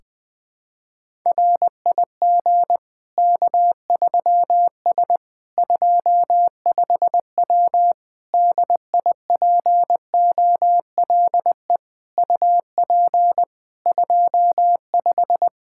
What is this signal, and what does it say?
Morse code, heard as a keyed tone, about 20 wpm, 700 Hz. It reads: RIG K3S 25W DIPOLE UP 25
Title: K1USN SST QSO Practice 20wpm